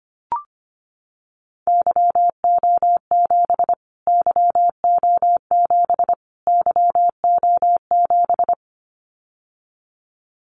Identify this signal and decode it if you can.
Morse code, heard as a keyed tone, about 25 wpm, 700 Hz. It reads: YO7 YO7 YO7
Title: Callsign Prefixes for Top 25 DXCC - Repeated 3x - 25wpm